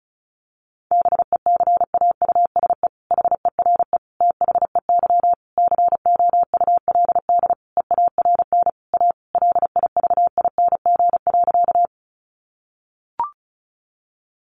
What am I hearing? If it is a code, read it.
Morse code, heard as a keyed tone, about 34 wpm, 700 Hz. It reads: BECAUSE HERE THEY COULD EARN A LIVING.
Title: Sentences from Top 1500 Words 35wpm